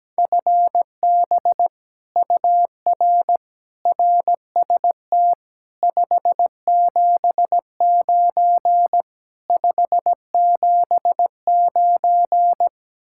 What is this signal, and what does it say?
Morse code, heard as a keyed tone, about 17 wpm, 700 Hz. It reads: FB UR RST 579 579
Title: Speed-Racing Common QSO Phrases - 17wpm